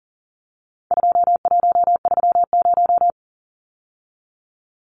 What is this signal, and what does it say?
Morse code, heard as a keyed tone, about 40 wpm, 700 Hz. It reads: 2130
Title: Numbers - 4-Digit 40wpm